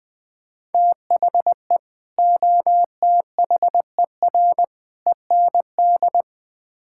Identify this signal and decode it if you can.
Morse code, heard as a keyed tone, about 20 wpm, 700 Hz. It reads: THE OTHER END
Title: Build A Phrase - 5-Word Phrases - 20wpm